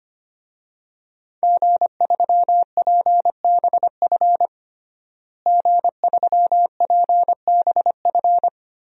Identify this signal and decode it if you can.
Morse code, heard as a keyed tone, about 25 wpm, 700 Hz. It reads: G3PBF G3PBF
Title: DXCC-100 Call Signs - 5 Characters - Repeated 3x - 25wpm